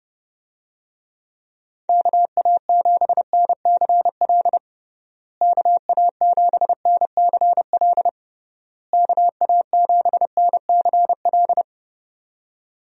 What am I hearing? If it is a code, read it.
Morse code, heard as a keyed tone, about 30 wpm, 700 Hz. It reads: KA7NCL KA7NCL KA7NCL
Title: USA Call Signs - 6 Characters - Repeated 3x - 30wpm